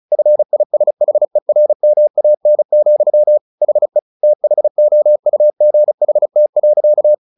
Morse code, heard as a keyed tone, about 34 wpm, 600 Hz. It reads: FISHERMAN, HE THOUGHT.
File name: the old man and the sea / the old man and the sea 1 35wpm